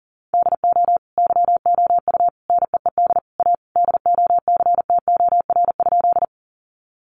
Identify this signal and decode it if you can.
Morse code, heard as a keyed tone, about 37 wpm, 700 Hz. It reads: DO YOU NEED A DOCTOR?
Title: Sentences from Top 1500 Words 40wpm